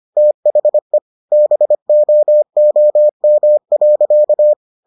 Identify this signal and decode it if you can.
Morse code, heard as a keyed tone, about 25 wpm, 600 Hz. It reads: THE BOOM.